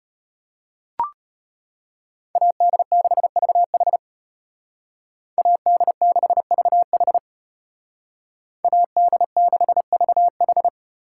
Morse code, heard as a keyed tone, about 33 wpm, 700 Hz. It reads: AD6VH AD6VH AD6VH